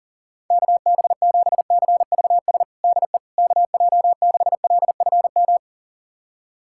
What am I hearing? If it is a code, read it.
Morse code, heard as a keyed tone, about 40 wpm, 690 Hz. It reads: KB7CVS DE KJ6LFM
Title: USA Callsign DE Callsign - 6 Characters - 40wpm